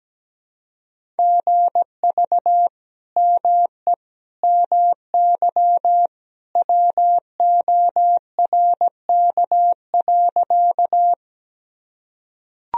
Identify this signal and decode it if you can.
Morse code, heard as a keyed tone, about 17 wpm, 700 Hz. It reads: GV ME MY WORK.
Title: Sentences from Top 200 Words and Abbreviations 17wpm